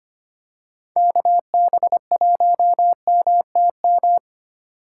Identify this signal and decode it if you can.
Morse code, heard as a keyed tone, about 25 wpm, 700 Hz. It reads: KB1MTM
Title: USA Call Signs - 4x Character Spacing - 25wpm